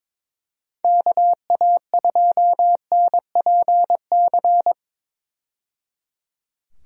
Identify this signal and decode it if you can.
Morse code, heard as a keyed tone, about 22 wpm, 700 Hz. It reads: KA2NPC
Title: USA Call Signs - 6 Characters - 22wpm